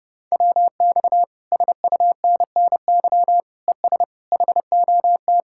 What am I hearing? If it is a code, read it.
Morse code, heard as a keyed tone, about 30 wpm, 700 Hz. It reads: WX SUNNY ES HOT